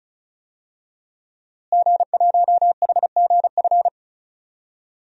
Morse code, heard as a keyed tone, about 34 wpm, 700 Hz. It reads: G1HGF
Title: DXCC-100 Call Signs - 5 Characters - Repeated 2x - 35wpm